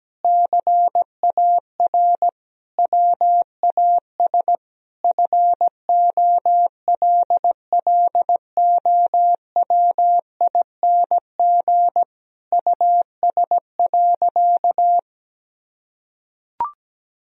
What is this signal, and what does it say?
Morse code, heard as a keyed tone, about 17 wpm, 700 Hz. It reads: CAR WAS FOLLOWING US.
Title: Sentences from Top 1000 Words - Review Part 4 - 17wpm